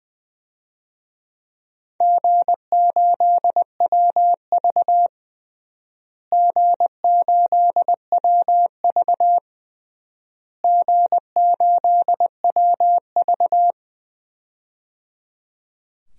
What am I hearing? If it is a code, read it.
Morse code, heard as a keyed tone, about 20 wpm, 700 Hz. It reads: G8WV G8WV G8WV
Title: DXCC-100 Call Signs - 4 Characters - Repeated 3x - 20wpm